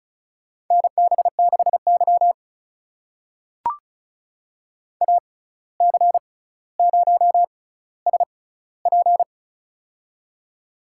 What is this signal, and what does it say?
Morse code, heard as a keyed tone, about 36 wpm, spaced as for 17 wpm, 700 Hz. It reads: NB6Y AC0SP
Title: USA Call Signs - 6x Character Spacing - 35wpm